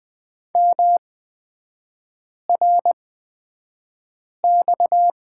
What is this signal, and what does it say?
Morse code, heard as a keyed tone, about 20 wpm, spaced as for 5 wpm, 690 Hz. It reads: MRX
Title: 229 Lesson 37e - Letter x - Three Characters 20wpm